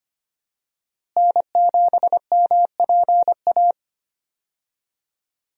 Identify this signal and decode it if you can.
Morse code, heard as a keyed tone, about 25 wpm, 700 Hz. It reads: N7MPA
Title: USA Call Signs - 5 Characters - Repeated 3x - 25wpm